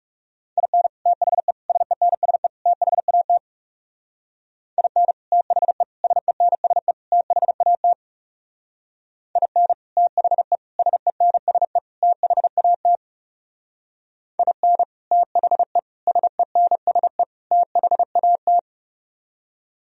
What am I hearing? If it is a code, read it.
Morse code, heard as a keyed tone, about 40 wpm, 700 Hz. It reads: IN THE SENSE THAT IN THE SENSE THAT IN THE SENSE THAT IN THE SENSE THAT